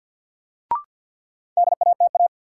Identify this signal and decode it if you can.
Morse code, heard as a keyed tone, about 51 wpm, 700 Hz. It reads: DATA
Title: Sentences from Top 1000 Words - Review Part 3 - New Word Drill - 50wpm